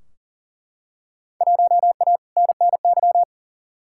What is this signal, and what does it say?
Morse code, heard as a keyed tone, about 40 wpm, 700 Hz. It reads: 1A NNY
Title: Field Day Exchanges Encore - 40wpm